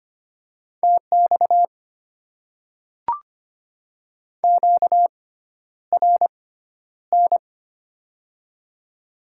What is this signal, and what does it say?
Morse code, heard as a keyed tone, about 25 wpm, spaced as for 11 wpm, 690 Hz. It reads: TX QRN
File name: Top 90 QSO Elements - 6x Character Spacing - 25wpm